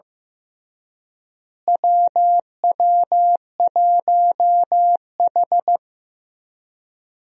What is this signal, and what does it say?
Morse code, heard as a keyed tone, about 15 wpm, 700 Hz. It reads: WW1H
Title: USA Call Signs - 4 Characters - Repeated 3x - 15wpm